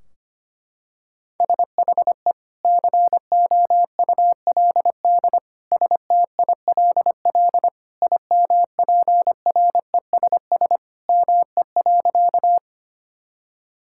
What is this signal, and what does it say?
Morse code, heard as a keyed tone, about 25 wpm, 700 Hz. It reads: SHE COULD STILL IMPRESS ME.
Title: Sentences from Top 1600 Words 25wpm